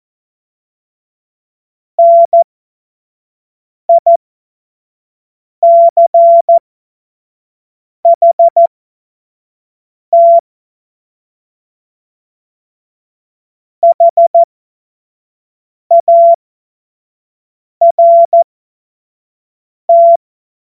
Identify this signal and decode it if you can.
Morse code, heard as a keyed tone, about 14 wpm, spaced as for 5 wpm, 680 Hz. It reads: NICHT HART
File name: Morsekurs / Witze / Witz 01. August 2023 25 F70